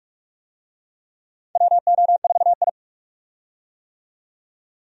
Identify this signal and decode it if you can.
Morse code, heard as a keyed tone, about 46 wpm, 700 Hz. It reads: WO4I
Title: USA Call Signs - 4 Characters - 45wpm